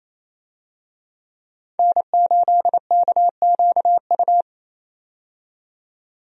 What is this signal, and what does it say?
Morse code, heard as a keyed tone, about 28 wpm, 700 Hz. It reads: N8KQU
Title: USA Call Signs - 5 Characters - 28wpm